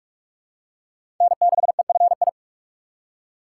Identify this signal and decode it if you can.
Morse code, heard as a keyed tone, about 43 wpm, 700 Hz. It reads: N6EFI
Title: USA Call Signs - 5 Characters - 6x Character Spacing - 45wpm